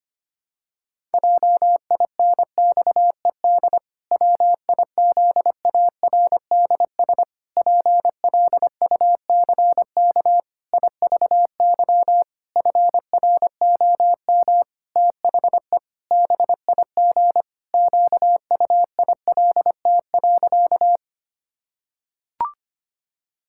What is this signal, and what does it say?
Morse code, heard as a keyed tone, about 25 wpm, 700 Hz. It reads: JINXED WIZARDS PLUCK IVY FROM THE BIG QUILT.
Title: Pangram Challenge v2 - 25wpm